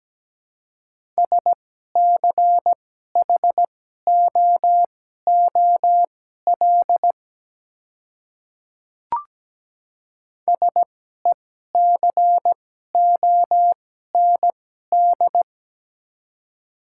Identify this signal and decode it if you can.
Morse code, heard as a keyed tone, about 17 wpm, spaced as for 12 wpm, 700 Hz. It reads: SCHOOL SECOND
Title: 2x Character Spacing - Top 300 Words - 17wpm